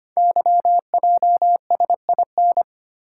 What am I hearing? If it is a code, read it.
Morse code, heard as a keyed tone, about 25 wpm, 700 Hz. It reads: YJSIN
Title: Code Groups - 5 Letters - 25wpm